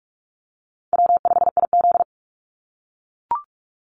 Morse code, heard as a keyed tone, about 44 wpm, 700 Hz. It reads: W5IZ
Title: Call Signs - 45wpm